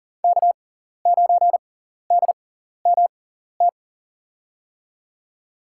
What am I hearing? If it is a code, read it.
Morse code, heard as a keyed tone, about 40 wpm, spaced as for 14 wpm, 700 Hz. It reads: K9DMT
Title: USA Call Signs - 5 Characters - 6x Character Spacing - 40wpm